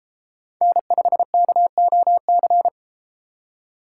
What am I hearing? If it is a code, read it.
Morse code, heard as a keyed tone, about 33 wpm, 700 Hz. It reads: N5KOC